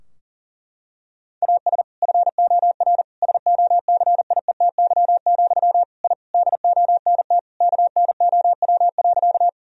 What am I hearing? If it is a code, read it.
Morse code, heard as a keyed tone, about 40 wpm, 700 Hz. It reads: AS FOR SOCIETY, I DONT KNOW.
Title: Sentences from Top 1000 Words - Review Part 4 - 40wpm